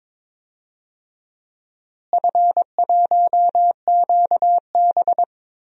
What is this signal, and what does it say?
Morse code, heard as a keyed tone, about 22 wpm, 700 Hz. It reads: F1QB
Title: DXCC-100 Call Signs - 4 Characters - Repeated 2x - 22wpm